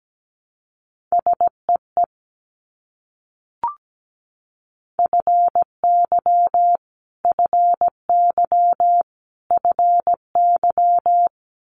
Morse code, heard as a keyed tone, about 17 wpm, 700 Hz. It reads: SEE FY FY FY
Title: Multi-letter Phonemes Repeated 3x 17wpm